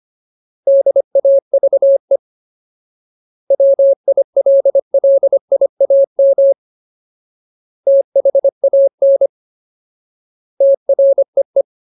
Morse code, heard as a keyed tone, about 25 wpm, 550 Hz. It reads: DAVE WILLIAM THAN TREE
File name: Ultimate Word List / 25WPM Ultimate